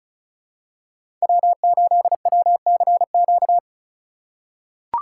Morse code, heard as a keyed tone, about 36 wpm, 700 Hz. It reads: W8WCQ
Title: USA Call Signs - 5 Characters - 4x Character Spacing - 35wpm